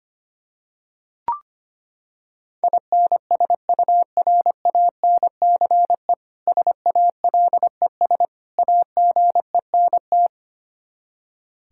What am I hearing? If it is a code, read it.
Morse code, heard as a keyed tone, about 25 wpm, 700 Hz. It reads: INSURANCE SALES AGENT
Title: K1USN SST QSO Practice 25wpm